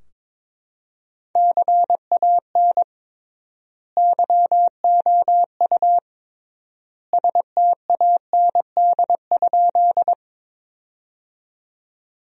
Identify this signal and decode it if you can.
Morse code, heard as a keyed tone, about 22 wpm, 700 Hz. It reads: CAN YOU STAND?
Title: 3x Word-Spacing - Sentences from Top 300 Words - 22wpm